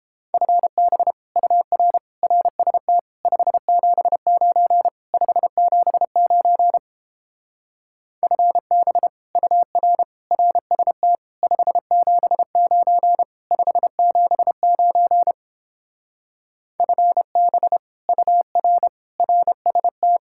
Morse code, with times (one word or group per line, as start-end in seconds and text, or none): 0.3-1.1 FB
1.4-2.0 UR
2.2-3.0 RST
3.2-4.9 579
5.1-6.8 579
8.2-9.1 FB
9.4-10.0 UR
10.3-11.2 RST
11.4-13.2 579
13.5-15.3 579
16.8-17.8 FB
18.1-18.9 UR
19.2-20.2 RST